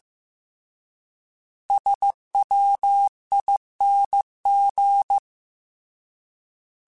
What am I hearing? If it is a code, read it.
Morse code, heard as a keyed tone, about 15 wpm, 800 Hz. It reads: SWING
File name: Misc / Misc-mp3s / Words @13